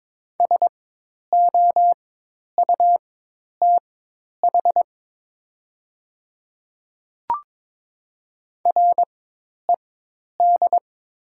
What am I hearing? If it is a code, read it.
Morse code, heard as a keyed tone, about 21 wpm, spaced as for 10 wpm, 700 Hz. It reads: SOUTH RED
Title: 4x Character Spacing - Top 500 Words - 22wpm